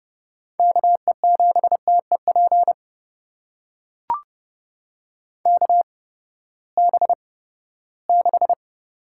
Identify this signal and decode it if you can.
Morse code, heard as a keyed tone, about 30 wpm, spaced as for 17 wpm, 700 Hz. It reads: KE7TEP KB6
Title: USA Call Signs - 6 Characters - 8x Character Spacing - 30wpm